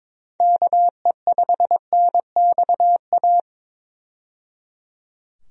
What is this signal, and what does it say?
Morse code, heard as a keyed tone, about 22 wpm, 700 Hz. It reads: KE5NXA
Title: USA Call Signs - 6 Characters - 22wpm